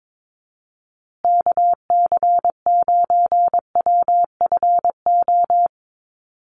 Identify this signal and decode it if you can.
Morse code, heard as a keyed tone, about 22 wpm, 700 Hz. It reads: KC9WFO
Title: USA Call Signs Repeated 3x 22wpm